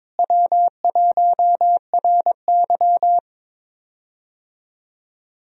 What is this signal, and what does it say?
Morse code, heard as a keyed tone, about 22 wpm, 700 Hz. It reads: W1RY
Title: USA Call Signs - 4 Characters - Repeated 3x - 22wpm